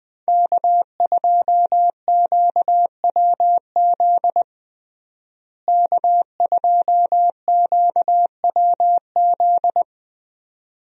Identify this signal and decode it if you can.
Morse code, heard as a keyed tone, about 21 wpm, 700 Hz. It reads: K2QWZ K2QWZ
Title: USA Call Signs - 5 Characters - Repeated 2x - 20wpm